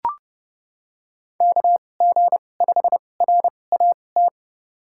Morse code, heard as a keyed tone, about 31 wpm, spaced as for 22 wpm, 700 Hz. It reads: KG5RAT
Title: USA Call Signs - 6 Characters - 2x Character Spacing - 30wpm